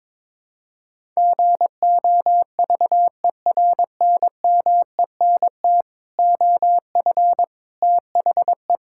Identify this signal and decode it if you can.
Morse code, heard as a keyed tone, about 21 wpm, 700 Hz. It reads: GOVERNMENT OF THE
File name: Sentences with 9-Word Phrases - 22wpm